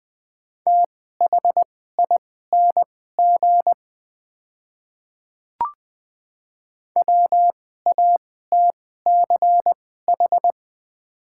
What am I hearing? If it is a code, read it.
Morse code, heard as a keyed tone, about 20 wpm, spaced as for 15 wpm, 700 Hz. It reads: THING WATCH
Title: 2x Character Spacing - Top 300 Words - 20wpm